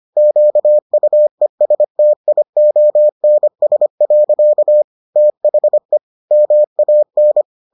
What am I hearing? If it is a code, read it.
Morse code, heard as a keyed tone, about 25 wpm, 600 Hz. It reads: QUESTIONS. THE MAN